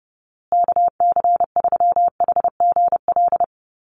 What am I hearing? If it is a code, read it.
Morse code, heard as a keyed tone, about 30 wpm, 700 Hz. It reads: KC3HGL
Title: USA Call Signs Repeated 2x 30wpm